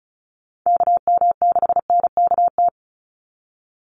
Morse code, heard as a keyed tone, about 36 wpm, 700 Hz. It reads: KM6NKT
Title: USA Call Signs 35wpm